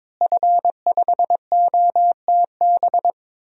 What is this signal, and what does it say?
Morse code, heard as a keyed tone, about 22 wpm, 700 Hz. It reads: F5OTB
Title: DXCC-100 Call Signs - 5 Characters - Repeated 3x - 22wpm